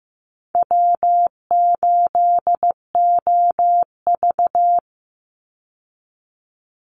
Morse code, heard as a keyed tone, about 15 wpm, 700 Hz. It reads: W8OV
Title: Easy Call Signs Repeated 2x - 15wpm